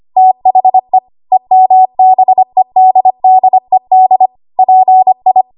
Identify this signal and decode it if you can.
Morse code, heard as a keyed tone, about 25 wpm, 750 Hz. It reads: THE EMBEDDED PI
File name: Archive / 25 WPM / 25WPM